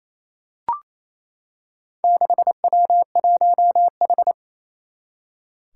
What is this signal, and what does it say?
Morse code, heard as a keyed tone, about 28 wpm, 700 Hz. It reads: 6W1H
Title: Four Challenging Letter-Number Combinations 28wpm